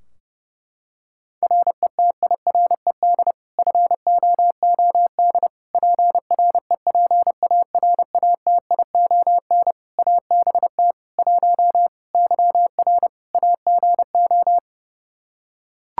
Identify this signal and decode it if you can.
Morse code, heard as a keyed tone, about 30 wpm, 700 Hz. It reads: RETIRED FOOD PREPARATION ABT 1 YR AGO
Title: Practice QSO - Exchange 3 - More Info - 30wpm